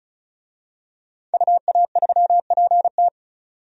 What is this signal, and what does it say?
Morse code, heard as a keyed tone, about 36 wpm, 700 Hz. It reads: UA3PT
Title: DXCC-100 Call Signs - 5 Characters - 35wpm